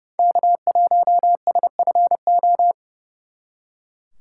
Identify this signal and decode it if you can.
Morse code, heard as a keyed tone, about 38 wpm, 700 Hz. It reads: K1SFO